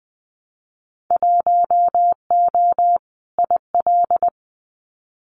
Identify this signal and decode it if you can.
Morse code, heard as a keyed tone, about 19 wpm, 700 Hz. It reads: 1O IL
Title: Winter Field Day Exchanges 20wpm